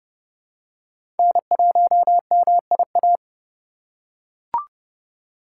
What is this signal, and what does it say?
Morse code, heard as a keyed tone, about 30 wpm, 700 Hz. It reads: N1MIA
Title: USA Call Signs - 5 Characters - 2x Character Spacing - 30wpm